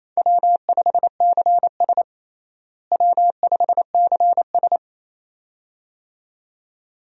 Morse code, heard as a keyed tone, about 28 wpm, 700 Hz. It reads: W5CS W5CS
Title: USA Call Signs - 4 Characters - Repeated 2x - 28wpm